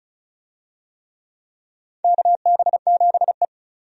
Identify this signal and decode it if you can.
Morse code, heard as a keyed tone, about 34 wpm, 700 Hz. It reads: KB7E
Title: USA Call Signs - 4 Characters - 35wpm